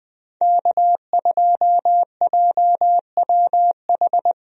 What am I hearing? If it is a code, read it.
Morse code, heard as a keyed tone, about 20 wpm, 700 Hz. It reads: K2JWH
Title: USA Call Signs - 5 Characters - Repeated 2x - 20wpm